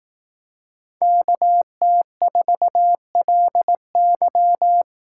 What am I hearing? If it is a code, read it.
Morse code, heard as a keyed tone, about 18 wpm, 700 Hz. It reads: KT4LY
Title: Speed Racing - USA 2x2 Call Signs 15wpm